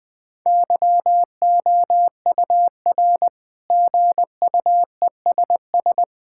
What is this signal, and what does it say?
Morse code, heard as a keyed tone, about 20 wpm, 700 Hz. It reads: YOUR GUESS
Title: Sentences with 7-Word Phrases - Spoken Beginning - 20wpm